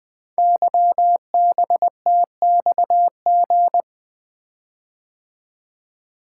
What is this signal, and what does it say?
Morse code, heard as a keyed tone, about 21 wpm, 700 Hz. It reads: YBTXG
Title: Code Groups - 5 Letters - 20wpm